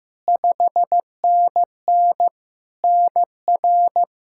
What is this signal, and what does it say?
Morse code, heard as a keyed tone, about 15 wpm, 700 Hz. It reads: 5NN NR
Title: Practice QSO - Exchange 1 - Beginning - 15wpm